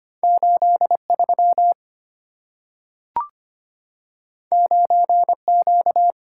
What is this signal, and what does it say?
Morse code, heard as a keyed tone, about 25 wpm, 700 Hz. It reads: 83 9Q
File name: Two Letter-Number Combinations - 25wpm